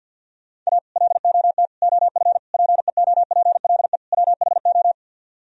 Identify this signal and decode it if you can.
Morse code, heard as a keyed tone, about 64 wpm, 700 Hz. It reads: A LOT OF PEOPLE WHO